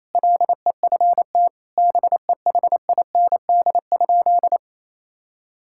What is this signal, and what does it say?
Morse code, heard as a keyed tone, about 28 wpm, 700 Hz. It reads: LEFT BEHIND?
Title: Sentences from Top 1000 Words - Review Part 5 - 28wpm